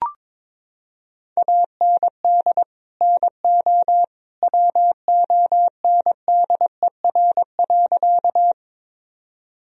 Morse code, heard as a keyed tone, about 22 wpm, 700 Hz. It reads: AND NO WONDER.
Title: Sentences from Top 1000 Words - Review Part 3 - 22wpm